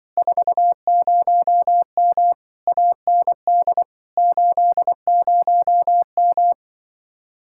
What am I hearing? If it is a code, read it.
Morse code, heard as a keyed tone, about 23 wpm, 700 Hz. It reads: 40M AND 80M
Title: Speed-Racing Common QSO Phrases - 20wpm